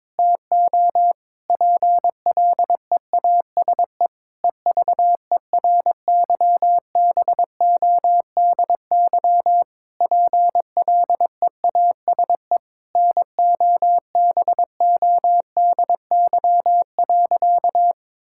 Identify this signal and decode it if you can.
Morse code, heard as a keyed tone, about 22 wpm, 700 Hz. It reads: TO PLEASE EVERYBODY PLEASE NOBODY.
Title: Sentences from Top 1600 Words 22wpm